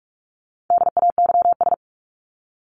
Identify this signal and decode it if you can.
Morse code, heard as a keyed tone, about 44 wpm, 700 Hz. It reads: DAYS
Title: ES:S Suffix - Top 100 Words with Base Words 45wpm